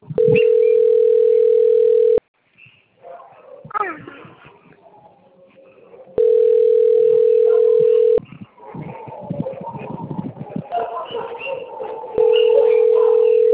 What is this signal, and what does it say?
Ringback tone (North America), 440 + 480 Hz.